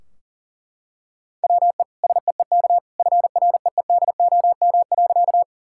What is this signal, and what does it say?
Morse code, heard as a keyed tone, about 40 wpm, 700 Hz. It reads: WE SEEK FREEDOM.
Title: Sentences from Top 1600 Words 40wpm